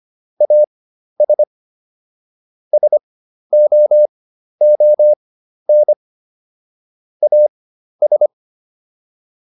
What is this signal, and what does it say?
Morse code, heard as a keyed tone, about 25 wpm, spaced as for 12 wpm, 600 Hz. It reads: AS SOON AS